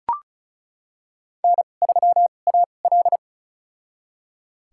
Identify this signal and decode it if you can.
Morse code, heard as a keyed tone, about 34 wpm, spaced as for 25 wpm, 700 Hz. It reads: N3AL
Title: USA Call Signs - 4 Characters - 2x Character Spacing - 35wpm